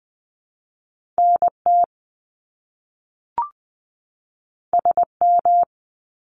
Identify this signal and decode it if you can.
Morse code, heard as a keyed tone, about 20 wpm, 700 Hz. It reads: N T S M
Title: Top Two Letter Combinations in Words 20wpm